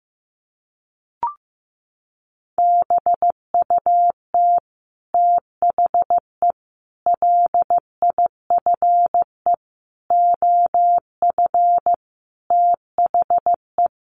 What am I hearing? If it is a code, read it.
Morse code, heard as a keyed tone, about 14 wpm, 700 Hz. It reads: BUT THE LIFE OF THE
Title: Sentences from Top 1500 Words 15wpm